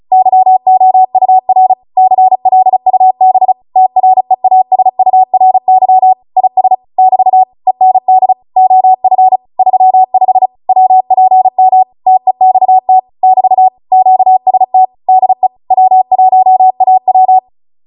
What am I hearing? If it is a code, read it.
Morse code, heard as a keyed tone, about 36 wpm, 750 Hz. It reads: YOUR CLUB TREASURY IS <BT> END OF 35 WPM TEXT <BT> QST DE W1AW
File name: Morse / 35WPM